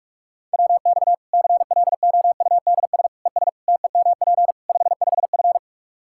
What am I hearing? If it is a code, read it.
Morse code, heard as a keyed tone, about 46 wpm, 700 Hz. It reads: WX CLOUDS ES TEMP 55F
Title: K1USN SST QSO Practice 45wpm